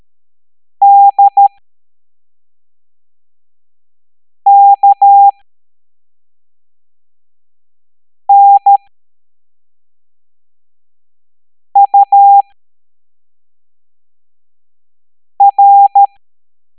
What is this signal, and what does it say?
Morse code, heard as a keyed tone, about 13 wpm, spaced as for 3 wpm, 800 Hz. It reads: DKNUR